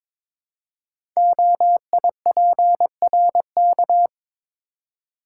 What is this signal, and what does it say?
Morse code, heard as a keyed tone, about 22 wpm, 700 Hz. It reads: OIPRK
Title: Code Groups - 5 Letters - 22wpm